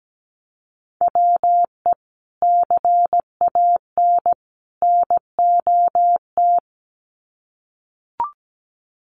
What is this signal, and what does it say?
Morse code, heard as a keyed tone, about 17 wpm, 700 Hz. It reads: WE CAN NOT
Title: Sets of 3 Words 17wpm